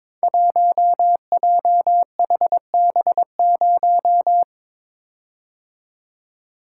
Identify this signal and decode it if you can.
Morse code, heard as a keyed tone, about 22 wpm, 700 Hz. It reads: 1JHB0
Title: Code Groups - 5 Letter:Numbers - 22wpm